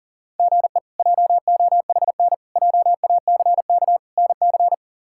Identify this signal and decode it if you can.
Morse code, heard as a keyed tone, about 37 wpm, 700 Hz. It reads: GE JOHN JACK NC
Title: K1USN SST QSO Practice - 40wpm